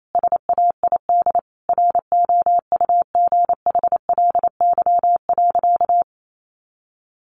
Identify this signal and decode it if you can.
Morse code, heard as a keyed tone, about 28 wpm, 700 Hz. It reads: SAID ROUGHLY.
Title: Sentences from Top 1500 Words 28wpm